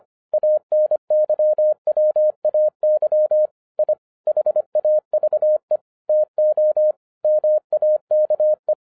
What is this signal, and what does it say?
Morse code, heard as a keyed tone, about 26 wpm, 610 Hz. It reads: ANYWAY I HAVE TO MAKE
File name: 2011-10-mp3 / n04-JUST MOVED 25wpm